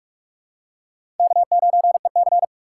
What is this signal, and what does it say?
Morse code, heard as a keyed tone, about 47 wpm, 700 Hz. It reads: K9EC
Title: USA Call Signs - 4 Characters - 2x Character Spacing - 45wpm